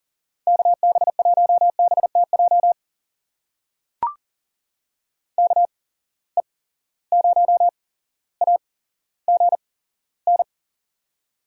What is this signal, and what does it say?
Morse code, heard as a keyed tone, about 38 wpm, spaced as for 17 wpm, 700 Hz. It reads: KB1BTJ KE0AGN
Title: USA Call Signs - 6 Characters - 8x Character Spacing - 40wpm